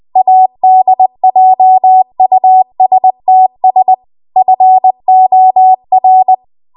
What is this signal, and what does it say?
Morse code, heard as a keyed tone, about 20 wpm, 750 Hz. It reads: ADJUSTS FOR
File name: Archive / 20 WPM / 20WPM